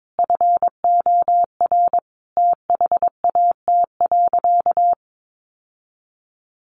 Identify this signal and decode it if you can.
Morse code, heard as a keyed tone, about 22 wpm, 700 Hz. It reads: FOR THAT.
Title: Sentences from Top 1100 Words 22wpm